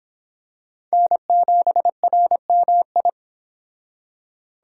Morse code, heard as a keyed tone, about 25 wpm, 700 Hz. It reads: N7RMI